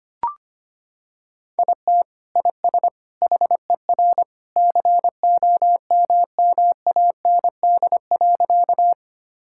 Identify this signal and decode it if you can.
Morse code, heard as a keyed tone, about 25 wpm, 700 Hz. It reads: IT IS HER COMMAND.
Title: Sentences from Top 1600 Words 25wpm